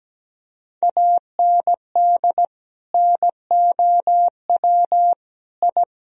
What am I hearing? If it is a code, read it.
Morse code, heard as a keyed tone, about 17 wpm, 700 Hz. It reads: AND NOW I
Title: Sentences from Top 1000 Words - Review Part 3 - 17wpm